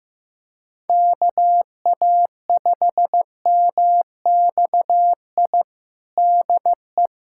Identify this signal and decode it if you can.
Morse code, heard as a keyed tone, about 15 wpm, 700 Hz. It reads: KA5MXI DE
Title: USA Callsign DE Callsign - 6 Characters - 15wpm